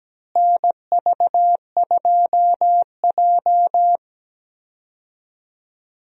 Morse code, heard as a keyed tone, about 17 wpm, 700 Hz. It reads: NV2J
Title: USA Call Signs - 4 Characters - Repeated 3x - 17wpm